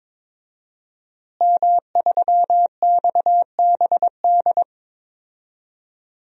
Morse code, heard as a keyed tone, about 22 wpm, 700 Hz. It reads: M3XBD
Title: DXCC-100 Call Signs - 5 Characters - Repeated 3x - 22wpm